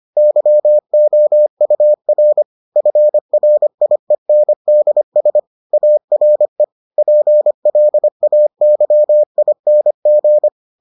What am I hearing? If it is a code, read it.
Morse code, heard as a keyed tone, about 25 wpm, 600 Hz. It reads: YOUR FRIENDS ARE PLAYING